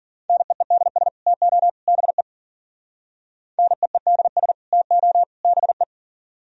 Morse code, heard as a keyed tone, about 51 wpm, 700 Hz. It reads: NEEDS TO BE NEEDS TO BE